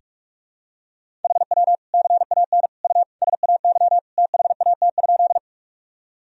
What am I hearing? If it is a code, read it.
Morse code, heard as a keyed tone, about 46 wpm, 700 Hz. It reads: HW CAN U SAY THAT?
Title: Sentences from Top 200 Words and Abbreviations 45wpm